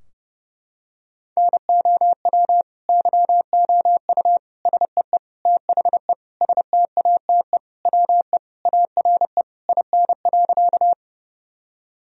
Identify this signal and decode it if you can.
Morse code, heard as a keyed tone, about 30 wpm, 700 Hz. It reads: NOW YOU SEE THE STATE WE ARE IN.
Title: Sentences from Top 1000 Words - Review Part 2 - 30wpm